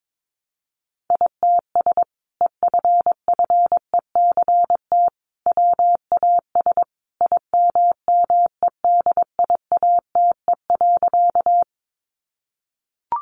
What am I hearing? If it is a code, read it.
Morse code, heard as a keyed tone, about 22 wpm, 700 Hz. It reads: ITS EFFECT WAS IMMEDIATE.
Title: Sentences from Top 1200 Words 22wpm